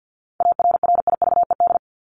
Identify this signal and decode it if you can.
Morse code, heard as a keyed tone, about 48 wpm, 700 Hz. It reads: ARRIVED
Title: ED Suffix - Top 100 Words with Base Words 50wpm